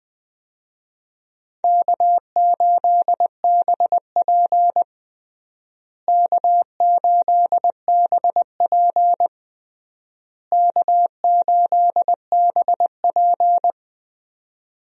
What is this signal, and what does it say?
Morse code, heard as a keyed tone, about 21 wpm, 700 Hz. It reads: K8BP K8BP K8BP
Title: USA Call Signs - 4 Characters - Repeated 3x - 20wpm